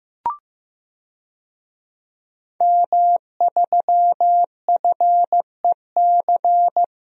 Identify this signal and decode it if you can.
Morse code, heard as a keyed tone, about 16 wpm, 700 Hz. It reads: M3FEC
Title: DXCC-100 Call Signs - 5 Characters - Repeated 2x - 15wpm